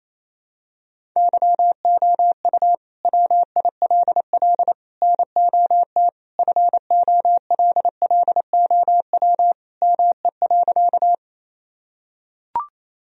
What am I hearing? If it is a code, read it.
Morse code, heard as a keyed tone, about 28 wpm, 700 Hz. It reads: YOU WILL NOT FOLLOW ME.
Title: Sentences from Top 1000 Words - Review Part 1 - 28wpm